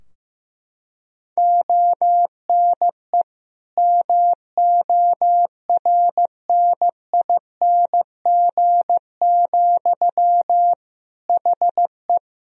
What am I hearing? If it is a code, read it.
Morse code, heard as a keyed tone, about 16 wpm, 700 Hz. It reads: ONE MORNING, HE
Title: Sentences from Top 1000 Words - Review Part 2 - 15wpm